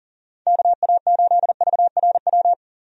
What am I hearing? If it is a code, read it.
Morse code, heard as a keyed tone, about 40 wpm, 690 Hz. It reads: KA8VRW
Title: USA Call Signs - 6 Characters - 40wpm